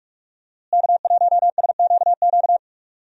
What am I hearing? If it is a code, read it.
Morse code, heard as a keyed tone, about 46 wpm, 700 Hz. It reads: K1SQQ
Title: USA Call Signs - 5 Characters - Repeated 3x - 45wpm